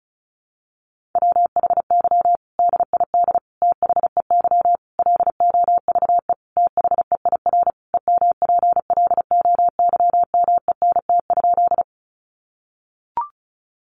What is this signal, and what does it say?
Morse code, heard as a keyed tone, about 34 wpm, 700 Hz. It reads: WHY DID THEY LOVE THEIR EMPLOYMENT?
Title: Sentences from Top 1500 Words 35wpm